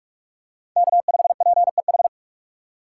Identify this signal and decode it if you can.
Morse code, heard as a keyed tone, about 46 wpm, 700 Hz. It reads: K5PEH
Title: USA Call Signs - 5 Characters - 3x Character Spacing - 45wpm